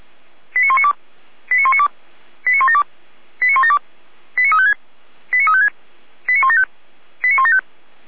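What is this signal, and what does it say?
Five-tone selective call (ZVEI), heard as quick runs of five tones, tones of about 80 ms. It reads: 89192 89192 89172 89172 89356 89356 89175 89175
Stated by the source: ZVEI five-tone selective call received with an SDR, archived by the Feuerwehr Orlamünde fire brigade.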